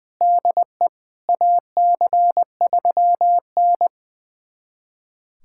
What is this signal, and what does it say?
Morse code, heard as a keyed tone, about 21 wpm, 700 Hz. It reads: DE AC3N
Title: USA Callsign DE Callsign - 4 Characters - 20wpm